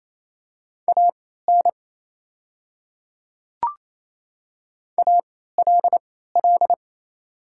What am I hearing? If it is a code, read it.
Morse code, heard as a keyed tone, about 27 wpm, spaced as for 16 wpm, 690 Hz. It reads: AN ALL
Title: 3x Character Spacing - Top 100 Words - 28wpm